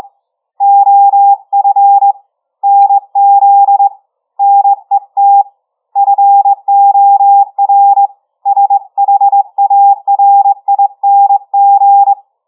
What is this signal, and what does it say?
Morse code, heard as a keyed tone, about 23 wpm, 800 Hz. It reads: OF NZ NET FOR SHARING
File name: challenge-133